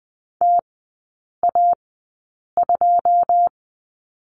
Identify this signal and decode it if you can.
Morse code, heard as a keyed tone, about 20 wpm, spaced as for 8 wpm, 700 Hz. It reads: TA2
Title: 081 Lesson 14e - Number 2 - Three Characters 20wpm